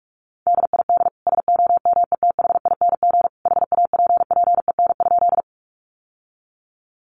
Morse code, heard as a keyed tone, about 44 wpm, 700 Hz. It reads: DID SOMETHING HAPPEN?
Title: Sentences from Top 600 Words 45wpm